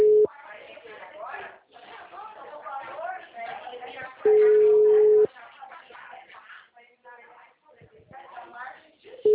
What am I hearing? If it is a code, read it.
Ringback tone (Europe, 425 Hz), with speech.
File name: out- -2026- - - .64174